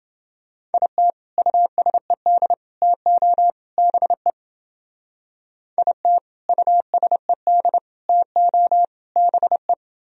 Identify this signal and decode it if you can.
Morse code, heard as a keyed tone, about 31 wpm, 700 Hz. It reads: IT USED TO BE IT USED TO BE